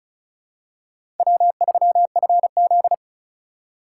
Morse code, heard as a keyed tone, about 36 wpm, 700 Hz. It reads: W3FZ
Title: USA Call Signs - 4x Character Spacing - 35wpm